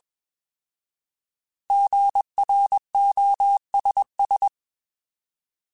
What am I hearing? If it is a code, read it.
Morse code, heard as a keyed tone, about 21 wpm, 800 Hz. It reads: GROSS
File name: Alphabetical / Alphabetical mp3s / Gs @20